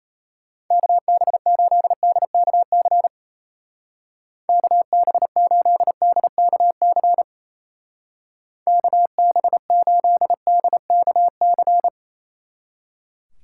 Morse code, heard as a keyed tone, about 36 wpm, 700 Hz. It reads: KB8DKC KB8DKC KB8DKC